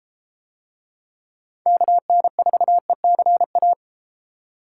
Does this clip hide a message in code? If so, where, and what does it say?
Yes: Morse code from 1.7 to 3.7 s, heard as a keyed tone, about 31 wpm, 700 Hz. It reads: KN4ECA